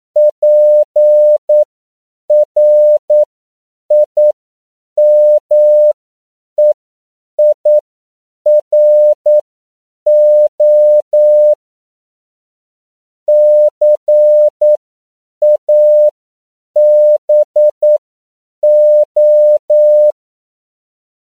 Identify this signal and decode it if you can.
Morse code, heard as a keyed tone, about 9 wpm, spaced as for 7 wpm, 600 Hz. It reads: PRIMEIRO CABO
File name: CW / 10PPM